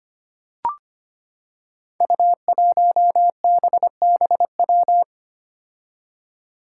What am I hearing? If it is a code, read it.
Morse code, heard as a keyed tone, about 25 wpm, 700 Hz. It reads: U1BBW
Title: Five Challenging Letter-Number Combinations 25wpm